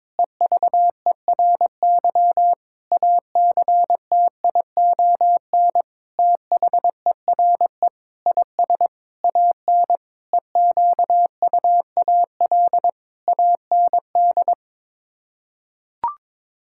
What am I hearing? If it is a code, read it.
Morse code, heard as a keyed tone, about 22 wpm, 700 Hz. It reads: EVERY ACTION THERE IS AN EQUAL AND
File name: practice / 22wpm / Sets of 8 Words - 22wpm